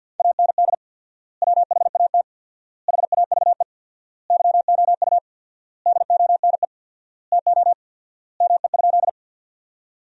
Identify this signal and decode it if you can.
Morse code, heard as a keyed tone, about 53 wpm, 700 Hz. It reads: AND WHAT HAVE YOU DONE TO ME?
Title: 4x Word-Spacing - Sentences from Top 500 Words - 50wpm